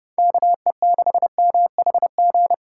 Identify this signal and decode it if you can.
Morse code, heard as a keyed tone, about 29 wpm, 700 Hz. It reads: KE6MHG